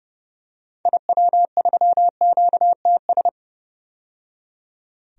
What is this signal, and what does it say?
Morse code, heard as a keyed tone, about 30 wpm, 700 Hz. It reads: IW3QTS
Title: DXCC-100 Call Signs - 6 Characters - 30wpm